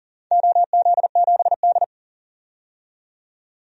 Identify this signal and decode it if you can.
Morse code, heard as a keyed tone, about 40 wpm, 700 Hz. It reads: OZ7D
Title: DXCC-100 Call Signs - 4 Characters - 40wpm